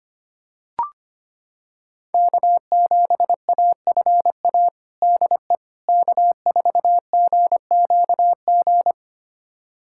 Morse code, heard as a keyed tone, about 25 wpm, 700 Hz. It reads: K7AFA DE K4GQG
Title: USA Callsign DE Callsign - 5 Characters - 25wpm